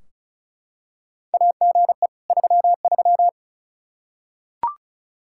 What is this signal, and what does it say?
Morse code, heard as a keyed tone, about 36 wpm, 700 Hz. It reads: AGE 33
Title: K1USN SST QSO Practice 35wpm